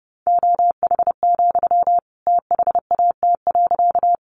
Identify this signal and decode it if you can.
Morse code, heard as a keyed tone, about 30 wpm, 700 Hz. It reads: OH, THAT.
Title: Sentences from Top 500 Words 30wpm